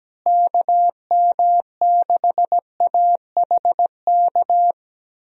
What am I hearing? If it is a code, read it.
Morse code, heard as a keyed tone, about 17 wpm, 700 Hz. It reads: KM6AHK
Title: USA Call Signs - 6 Characters - Repeated 3x - 17wpm